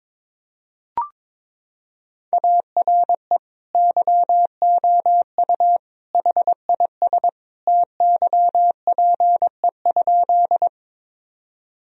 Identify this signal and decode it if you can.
Morse code, heard as a keyed tone, about 22 wpm, 700 Hz. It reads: ARE YOU HIS TYPE?
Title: Sentences from Top 1000 Words - Review Part 3 - 22wpm